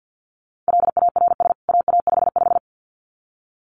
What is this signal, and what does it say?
Morse code, heard as a keyed tone, about 50 wpm, 700 Hz. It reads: LARS AA55
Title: CWT Exchange 50wpm